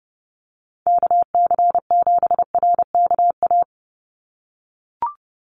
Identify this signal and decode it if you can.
Morse code, heard as a keyed tone, about 30 wpm, 700 Hz. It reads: KC7RKA
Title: USA Call Signs 30wpm